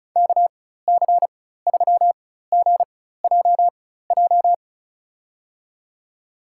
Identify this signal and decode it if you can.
Morse code, heard as a keyed tone, about 34 wpm, spaced as for 16 wpm, 700 Hz. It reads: KC3GJJ
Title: USA Call Signs - 6 Characters - 4x Character Spacing - 35wpm